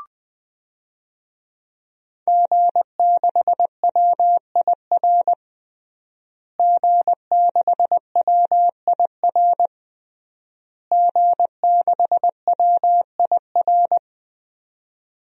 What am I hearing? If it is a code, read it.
Morse code, heard as a keyed tone, about 21 wpm, 700 Hz. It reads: G6WIR G6WIR G6WIR
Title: DXCC-100 Call Signs - 5 Characters - Repeated 3x - 20wpm